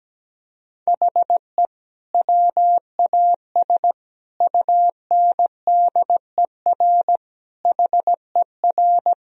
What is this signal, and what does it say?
Morse code, heard as a keyed tone, about 17 wpm, 700 Hz. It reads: HE WAS UNDER HER
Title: Sentences from Top 1000 Words - Review Part 1 - 17wpm